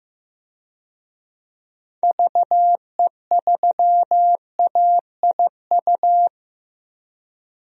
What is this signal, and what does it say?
Morse code, heard as a keyed tone, about 15 wpm, 700 Hz. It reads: VE3AIU
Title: DXCC-100 Call Signs - 6 Characters - Repeated 2x - 15wpm